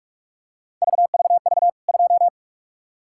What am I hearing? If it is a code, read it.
Morse code, heard as a keyed tone, about 43 wpm, 700 Hz. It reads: VVV 2
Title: K1USN SST QSO Practice 45wpm